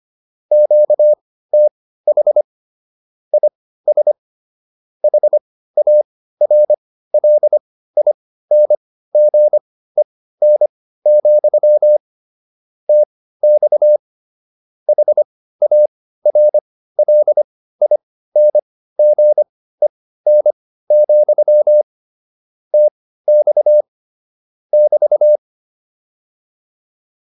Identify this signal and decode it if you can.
Morse code, heard as a keyed tone, about 25 wpm, spaced as for 15 wpm, 600 Hz. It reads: QTH IS HARLINGEN, TX HARLINGEN, TX <BT>
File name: QSO304 15